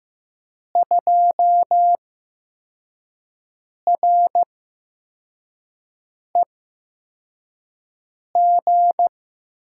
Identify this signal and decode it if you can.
Morse code, heard as a keyed tone, about 15 wpm, spaced as for 4 wpm, 700 Hz. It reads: 2REG
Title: USA Call Signs - 5 Characters - 8x Character Spacing - 15wpm